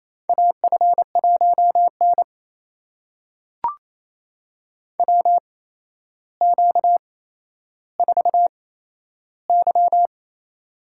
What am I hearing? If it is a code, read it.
Morse code, heard as a keyed tone, about 28 wpm, spaced as for 12 wpm, 700 Hz. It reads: AF1N WQ4Y
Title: USA Call Signs - 4 Characters - 8x Character Spacing - 28wpm